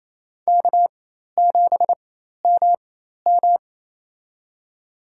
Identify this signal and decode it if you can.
Morse code, heard as a keyed tone, about 27 wpm, spaced as for 13 wpm, 700 Hz. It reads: K7MM
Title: USA Call Signs - 4 Characters - 4x Character Spacing - 28wpm